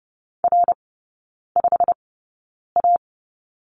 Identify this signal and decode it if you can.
Morse code, heard as a keyed tone, about 30 wpm, spaced as for 9 wpm, 700 Hz. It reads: R5A
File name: course / 30:10wpm / 087 Lesson 15e - Number 5 - Three Characters 30wpm